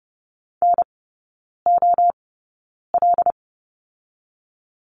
Morse code, heard as a keyed tone, about 31 wpm, spaced as for 9 wpm, 690 Hz. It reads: NOL
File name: Lesson 16e - Letter c - Three Characters 30wpm